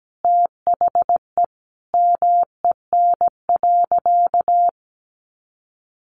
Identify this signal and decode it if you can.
Morse code, heard as a keyed tone, about 17 wpm, 700 Hz. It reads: THE MEN.
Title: Sentences from Top 1400 Words 17wpm